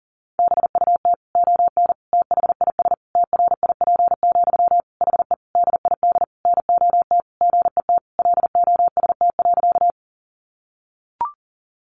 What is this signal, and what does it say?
Morse code, heard as a keyed tone, about 40 wpm, 700 Hz. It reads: BUT ON THIS TRIP, HE DID NOT GET LOST.
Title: Sentences from Top 1400 Words 40wpm